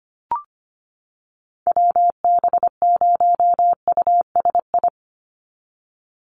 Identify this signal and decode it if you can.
Morse code, heard as a keyed tone, about 25 wpm, 700 Hz. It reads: WB0USI
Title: NA SOTA Activator Callsigns 25wpm